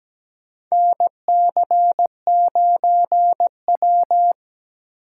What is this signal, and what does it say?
Morse code, heard as a keyed tone, about 17 wpm, 700 Hz. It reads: NC9W
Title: K1USN SST QSO Practice - 17wpm